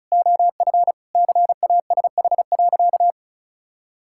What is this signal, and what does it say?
Morse code, heard as a keyed tone, about 36 wpm, 700 Hz. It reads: OF CASH.
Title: Sentences from Top 1000 Words - Review Part 5 - 35wpm